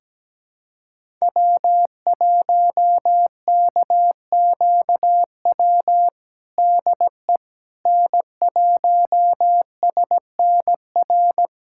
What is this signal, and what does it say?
Morse code, heard as a keyed tone, about 17 wpm, 700 Hz. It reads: W1KQW DE N1SNR
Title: USA Callsign DE Callsign - 5 Characters - 17wpm